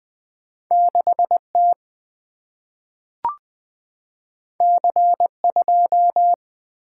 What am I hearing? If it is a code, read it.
Morse code, heard as a keyed tone, about 20 wpm, 690 Hz. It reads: 6T C2
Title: Two Letter-Number Combinations - 20wpm